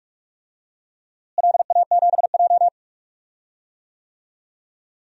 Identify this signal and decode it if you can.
Morse code, heard as a keyed tone, about 46 wpm, 700 Hz. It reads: LA7J
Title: DXCC-100 Call Signs - 4 Characters - Repeated 3x - 45wpm